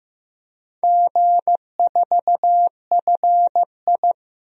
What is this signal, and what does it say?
Morse code, heard as a keyed tone, about 15 wpm, 700 Hz. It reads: G4FI
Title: DXCC-100 Call Signs - 4 Characters - 15wpm